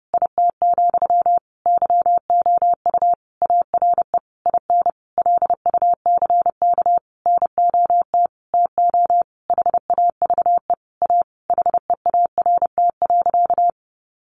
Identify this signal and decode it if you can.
Morse code, heard as a keyed tone, about 30 wpm, 700 Hz. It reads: IT, YOU ARE IN LUCK NOT TO HAVE A HEART.
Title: Sentences from Top 1500 Words 30wpm